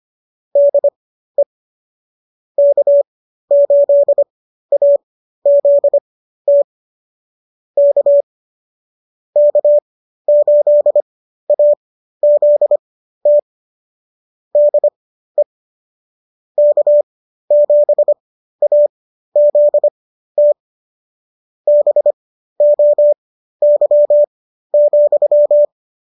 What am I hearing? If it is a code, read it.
Morse code, heard as a keyed tone, about 25 wpm, spaced as for 13 wpm, 590 Hz. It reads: DE K8AZT K K8AZT DE K7AZT BOY,